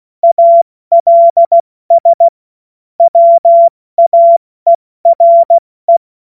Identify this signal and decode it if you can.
Morse code, heard as a keyed tone, about 16 wpm, 680 Hz. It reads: ALS WAERE
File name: Morsekurs / Witze / Witz 04. Juli 2023 70 F80